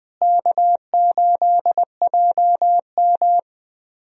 Morse code, heard as a keyed tone, about 21 wpm, 700 Hz. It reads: K8JM
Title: USA Call Signs - 4 Characters - Repeated 3x - 20wpm